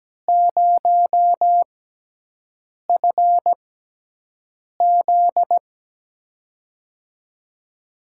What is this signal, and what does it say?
Morse code, heard as a keyed tone, about 17 wpm, spaced as for 6 wpm, 700 Hz. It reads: 0FZ
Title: USA Call Signs - 4 Characters - 6x Character Spacing - 17wpm